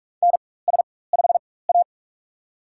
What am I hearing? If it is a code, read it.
Morse code, heard as a keyed tone, about 46 wpm, spaced as for 21 wpm, 690 Hz. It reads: NS5A
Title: USA Call Signs - 4x Character Spacing - 45wpm